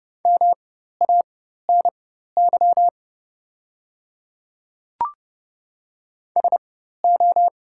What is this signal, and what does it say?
Morse code, heard as a keyed tone, about 30 wpm, spaced as for 14 wpm, 700 Hz. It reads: MANY SO
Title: 4x Character Spacing - Top 100 Words - 30wpm